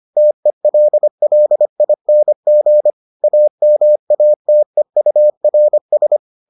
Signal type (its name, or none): Morse code